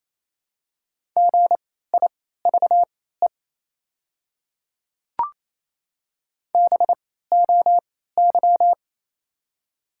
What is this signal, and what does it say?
Morse code, heard as a keyed tone, about 28 wpm, spaced as for 16 wpm, 700 Hz. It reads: GIVE BOY
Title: 3x Character Spacing - Top 200 Words - 28wpm